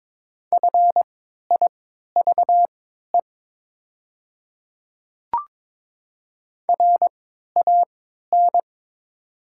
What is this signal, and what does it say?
Morse code, heard as a keyed tone, about 22 wpm, spaced as for 12 wpm, 700 Hz. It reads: FIVE RAN
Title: 3x Character Spacing - Top 500 Words - 22wpm